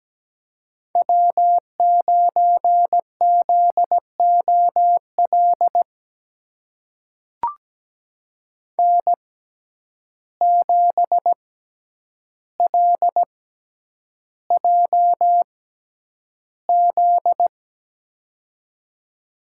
Morse code, heard as a keyed tone, about 17 wpm, spaced as for 9 wpm, 700 Hz. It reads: W9ZOL N7LJZ
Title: USA Call Signs - 6x Character Spacing - 17wpm